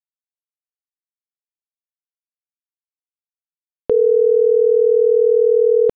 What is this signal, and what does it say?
Ringback tone (North America), 440 + 480 Hz.